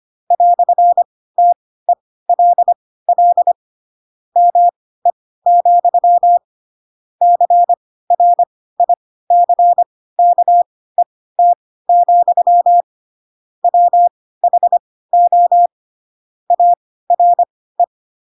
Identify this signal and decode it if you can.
Morse code, heard as a keyed tone, about 25 wpm, spaced as for 16 wpm, 700 Hz. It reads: "TELL ME, CRICKET, WHO ARE
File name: Pinocchio / Pinocchio ch4 10 25 ik1hge